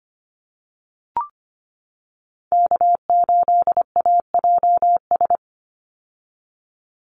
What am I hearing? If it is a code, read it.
Morse code, heard as a keyed tone, about 25 wpm, 700 Hz. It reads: K8AJS
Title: Call Signs - 25wpm